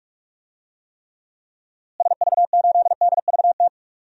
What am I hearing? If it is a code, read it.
Morse code, heard as a keyed tone, about 46 wpm, 700 Hz. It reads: SV8DVT
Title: DXCC-100 Call Signs - 6 Characters - Repeated 3x - 45wpm